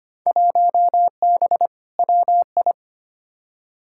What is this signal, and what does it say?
Morse code, heard as a keyed tone, about 25 wpm, 700 Hz. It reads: 1B WI
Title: Field Day Exchanges - 25wpm